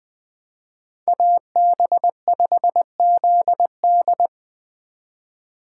Morse code, heard as a keyed tone, about 21 wpm, 700 Hz. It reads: AB5ZD
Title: Speed Racing - USA 2x2 Call Signs 17wpm